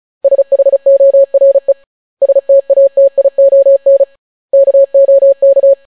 Morse code, heard as a keyed tone, about 36 wpm, 550 Hz. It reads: SHORE STATION KOK